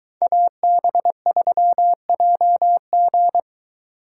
Morse code, heard as a keyed tone, about 22 wpm, 700 Hz. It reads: AB3JG